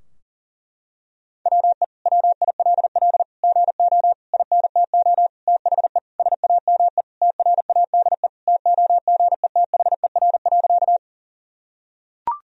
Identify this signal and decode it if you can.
Morse code, heard as a keyed tone, about 40 wpm, 700 Hz. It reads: WE WILL GO INTO THE SAME TRADE TOGETHER.
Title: Sentences from Top 1000 Words - Review Part 4 - 40wpm